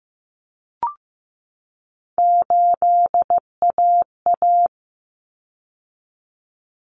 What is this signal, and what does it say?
Morse code, heard as a keyed tone, about 15 wpm, 700 Hz. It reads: 8AA
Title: Common Cut Numbers - 3-Digit 15wpm